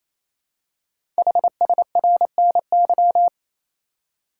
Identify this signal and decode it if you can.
Morse code, heard as a keyed tone, about 27 wpm, 700 Hz. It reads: HSRNY
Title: Code Groups - 5 Letters - 28wpm